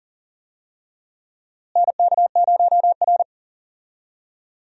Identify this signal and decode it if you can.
Morse code, heard as a keyed tone, about 39 wpm, 700 Hz. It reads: NK0R
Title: USA Call Signs - 4 Characters - Repeated 2x - 40wpm